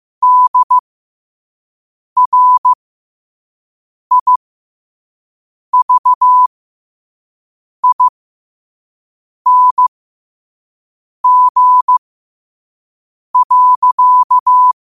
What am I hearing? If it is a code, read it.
Morse code, heard as a keyed tone, about 15 wpm, spaced as for 5 wpm, 1000 Hz. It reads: DRIVING.
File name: qotdfiles / 5wpm